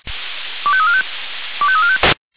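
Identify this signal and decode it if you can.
Five-tone selective call (ZVEI), heard as quick runs of five tones, tones of about 70 ms. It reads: 25345 25345
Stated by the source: ZVEI five-tone selective call received on an SDR.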